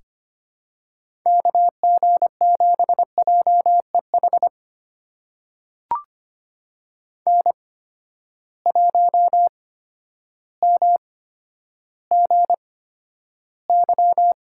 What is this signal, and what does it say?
Morse code, heard as a keyed tone, about 25 wpm, spaced as for 11 wpm, 700 Hz. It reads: KG7JEH N1MGY
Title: USA Call Signs - 8x Character Spacing - 25wpm